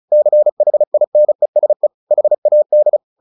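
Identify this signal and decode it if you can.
Morse code, heard as a keyed tone, about 34 wpm, 600 Hz. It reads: CHINESE HAD
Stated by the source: Morse practice at 35 wpm from CW med Gnister, sending Ken Follett's novel Never in English.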